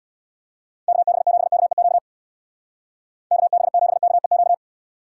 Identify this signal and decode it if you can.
Morse code, heard as a keyed tone, about 59 wpm, 700 Hz. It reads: KB7KEQ KB7KEQ
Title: Speed Racing - USA 2x3 Call Signs 50wpm